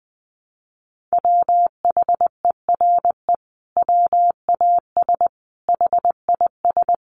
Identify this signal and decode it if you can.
Morse code, heard as a keyed tone, about 20 wpm, 700 Hz. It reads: WHERE WAS HIS
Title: Sentences from Top 1000 Words 20wpm